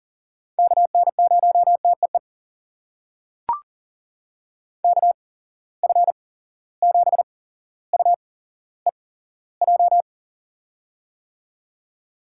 Morse code, heard as a keyed tone, about 40 wpm, spaced as for 17 wpm, 700 Hz. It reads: KN0TEE KF7UEJ
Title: USA Call Signs - 6 Characters - 8x Character Spacing - 40wpm